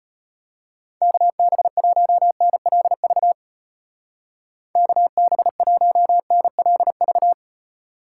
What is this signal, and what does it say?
Morse code, heard as a keyed tone, about 34 wpm, 700 Hz. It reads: KB1NLV KB1NLV